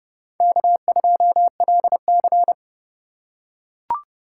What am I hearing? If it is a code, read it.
Morse code, heard as a keyed tone, about 30 wpm, 700 Hz. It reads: K2LC
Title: USA Call Signs - 4 Characters - 4x Character Spacing - 30wpm